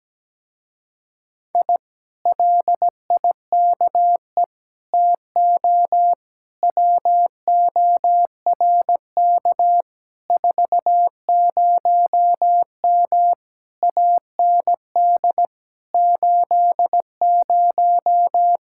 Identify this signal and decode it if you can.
Morse code, heard as a keyed tone, about 17 wpm, 700 Hz. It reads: I LIKE TO WORK 40M AND 80
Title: Speed-Racing Common QSO Phrases - 17wpm